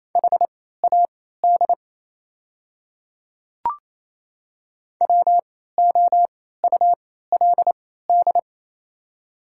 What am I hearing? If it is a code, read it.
Morse code, heard as a keyed tone, about 27 wpm, spaced as for 16 wpm, 700 Hz. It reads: HAD WOULD
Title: 3x Character Spacing - Top 300 Words - 28wpm